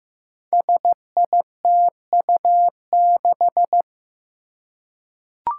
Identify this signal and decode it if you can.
Morse code, heard as a keyed tone, about 15 wpm, 700 Hz. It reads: SITU6
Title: Five Challenging Letter-Number Combinations 15wpm